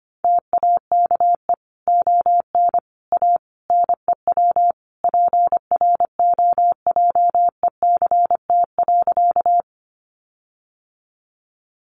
Morse code, heard as a keyed tone, about 25 wpm, 700 Hz. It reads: TAKE ON A NEW PROJECT.
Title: Sentences from Top 1400 Words 25wpm